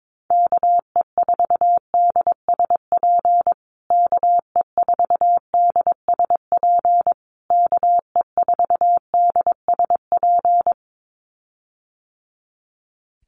Morse code, heard as a keyed tone, about 22 wpm, 700 Hz. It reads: KE4DSP KE4DSP KE4DSP
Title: USA Call Signs Repeated 3x 22wpm